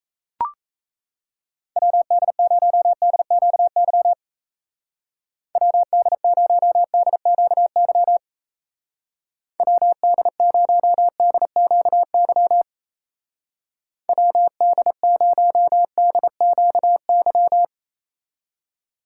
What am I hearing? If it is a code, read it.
Morse code, heard as a keyed tone, about 34 wpm, 700 Hz. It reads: WD0DQY WD0DQY WD0DQY WD0DQY
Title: Speed Racing - USA 2x3 Call Signs 28wpm